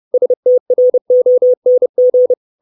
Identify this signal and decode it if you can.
Morse code, heard as a keyed tone, about 30 wpm, 500 Hz. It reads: STRONG
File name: 632 Words 30WPM